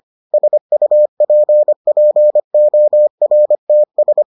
Morse code, heard as a keyed tone, about 25 wpm, 600 Hz. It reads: SUPPORTS